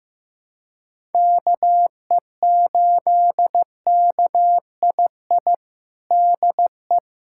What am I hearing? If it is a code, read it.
Morse code, heard as a keyed tone, about 16 wpm, 700 Hz. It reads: KE8KII DE
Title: USA Callsign DE Callsign - 6 Characters - 15wpm